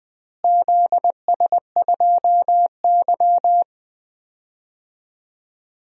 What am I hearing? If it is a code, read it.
Morse code, heard as a keyed tone, about 20 wpm, 690 Hz. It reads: ZS2Y
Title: DXCC-100 Call Signs - 4 Characters - Repeated 3x - 20wpm